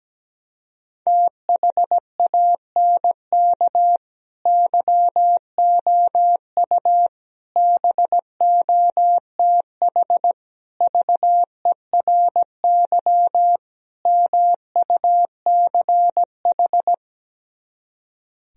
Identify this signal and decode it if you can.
Morse code, heard as a keyed tone, about 17 wpm, 700 Hz. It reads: THANK YOU BOTH VERY MUCH
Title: Speed-Racing - Sets of 5 Words 17wpm